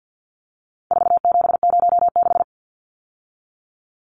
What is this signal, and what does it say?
Morse code, heard as a keyed tone, about 50 wpm, 700 Hz. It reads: 4706
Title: Numbers - 4-Digit 50wpm